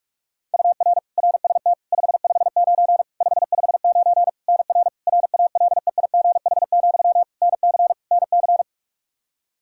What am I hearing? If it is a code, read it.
Morse code, heard as a keyed tone, about 46 wpm, 700 Hz. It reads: UR RST 559 559 NR RALEIGH, NC NC
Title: K1USN SST QSO Practice 45wpm